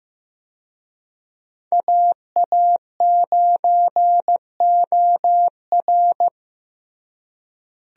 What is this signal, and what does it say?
Morse code, heard as a keyed tone, about 15 wpm, 700 Hz. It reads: AA9OR
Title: USA Call Signs - 5 Characters - Repeated 3x - 15wpm